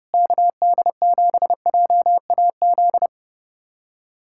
Morse code, heard as a keyed tone, about 30 wpm, 700 Hz. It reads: KD7JAZ
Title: USA Callsign DE Callsign - 6 Characters - 30wpm